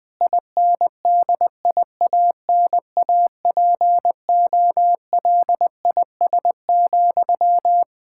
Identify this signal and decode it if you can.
Morse code, heard as a keyed tone, about 21 wpm, 700 Hz. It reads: INDIANAPOLIS,
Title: Practice QSO - Exchange 1 - Beginning - 20wpm